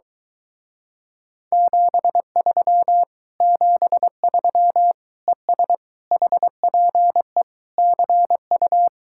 Morse code, heard as a keyed tone, about 23 wpm, 700 Hz. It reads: 73 73 ES HPE CU